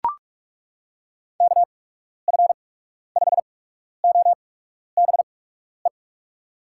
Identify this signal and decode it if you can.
Morse code, heard as a keyed tone, about 42 wpm, spaced as for 12 wpm, 700 Hz. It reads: KF5OBE
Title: USA Call Signs - 6 Characters - 8x Character Spacing - 45wpm